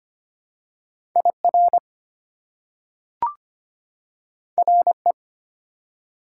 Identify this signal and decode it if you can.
Morse code, heard as a keyed tone, about 25 wpm, 690 Hz. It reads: IR RE
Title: Prefixes and Suffixes 25wpm